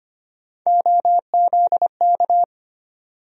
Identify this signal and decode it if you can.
Morse code, heard as a keyed tone, about 25 wpm, 700 Hz. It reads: OZK
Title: USA Call Sign Postfixes - 3 Letters 25wpm